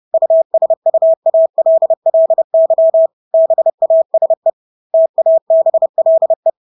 Morse code, heard as a keyed tone, about 30 wpm, 650 Hz. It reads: USUALLY BASE TABLE
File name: Engelske ord 30-30 wpm